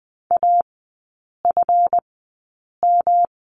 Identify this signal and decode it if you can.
Morse code, heard as a keyed tone, about 20 wpm, spaced as for 8 wpm, 690 Hz. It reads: AFM
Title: 212 Lesson 34e - Number 8 - Three Characters 20wpm